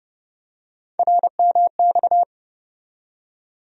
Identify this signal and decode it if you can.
Morse code, heard as a keyed tone, about 29 wpm, 700 Hz. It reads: RMX
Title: USA Call Sign Postfixes - 3 Letters 30wpm